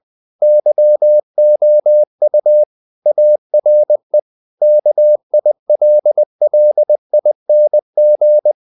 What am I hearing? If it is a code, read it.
Morse code, heard as a keyed tone, about 20 wpm, 600 Hz. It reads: YOU ARE KILLING
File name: the old man and the sea part 2 20wpm